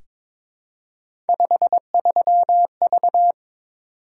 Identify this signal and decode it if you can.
Morse code, heard as a keyed tone, about 23 wpm, 700 Hz. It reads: 53V
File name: Three Letter-Number Combinations - 22wpm